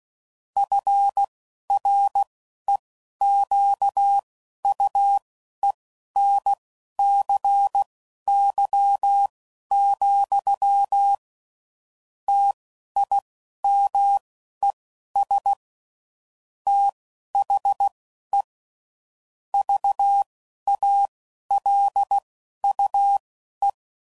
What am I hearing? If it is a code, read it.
Morse code, heard as a keyed tone, about 16 wpm, spaced as for 11 wpm, 800 Hz. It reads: FREQUENCY, TIMES THE VALUE
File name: CW Code Groups @10WPM / Technical Subjects / XC